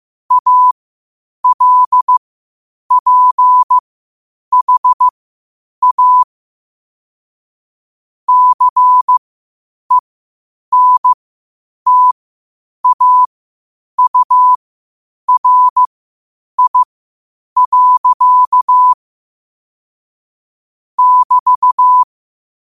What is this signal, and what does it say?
Morse code, heard as a keyed tone, about 15 wpm, spaced as for 8 wpm, 1000 Hz. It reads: ALPHA CENTAURI. <BT>
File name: qotdfiles / 8wpm